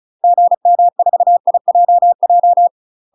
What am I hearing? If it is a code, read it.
Morse code, heard as a keyed tone, about 36 wpm, 700 Hz. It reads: GM4IJJ
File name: UK MASTER QSOs TRAINING / 35WPM / UK MASTER QSOs 0036